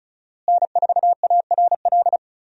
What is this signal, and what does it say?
Morse code, heard as a keyed tone, about 35 wpm, 700 Hz. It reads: N4ARL
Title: USA Call Signs - 5 Characters - Repeated 2x - 35wpm